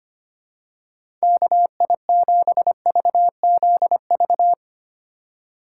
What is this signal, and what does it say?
Morse code, heard as a keyed tone, about 25 wpm, 700 Hz. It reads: KI7VZV
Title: USA Call Signs - 6 Characters - 3x Character Spacing - 25wpm